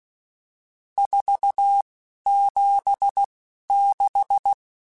Morse code, heard as a keyed tone, about 16 wpm, spaced as for 11 wpm, 800 Hz. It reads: 476
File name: RSTs / RST-mp3s / Various RST @10